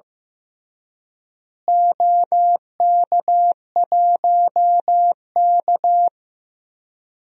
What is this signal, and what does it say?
Morse code, heard as a keyed tone, about 16 wpm, 700 Hz. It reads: OK1K
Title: DXCC-100 Call Signs - 4 Characters - Repeated 3x - 15wpm